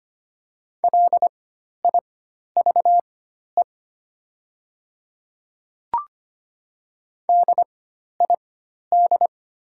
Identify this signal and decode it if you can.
Morse code, heard as a keyed tone, about 25 wpm, spaced as for 12 wpm, 690 Hz. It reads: LIVE DID
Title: 4x Character Spacing - Top 500 Words - 25wpm